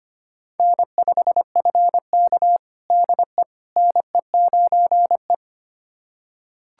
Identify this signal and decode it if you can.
Morse code, heard as a keyed tone, about 25 wpm, 700 Hz. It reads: N5FK DE NE9E
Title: USA Callsign DE Callsign - 4 Characters - 25wpm